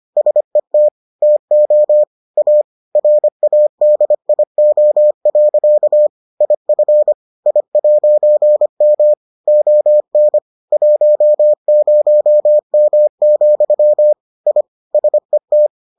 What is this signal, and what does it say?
Morse code, heard as a keyed tone, about 25 wpm, 600 Hz. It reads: SET TO A RADIO. IF I'M ON 10M, I SET